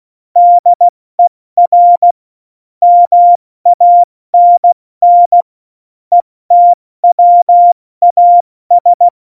Morse code, heard as a keyed tone, about 16 wpm, spaced as for 14 wpm, 700 Hz. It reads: DER MANN ETWAS
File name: Morsekurs / Witze / Witz 13 March 2018, 70 F80